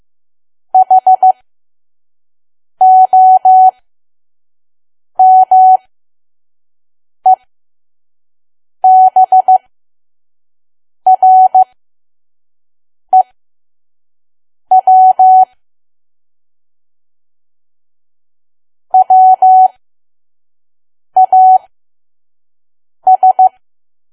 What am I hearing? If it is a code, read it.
Morse code, heard as a keyed tone, about 15 wpm, spaced as for 5 wpm, 750 Hz. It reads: HOMEBREW WAS
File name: Archive / 5 WPM files / PM